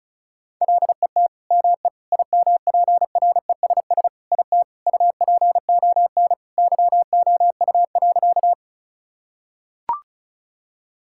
Morse code, heard as a keyed tone, about 34 wpm, 700 Hz. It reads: LET ME IMPRESS IT UPON YOU.
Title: Sentences from Top 1600 Words 35wpm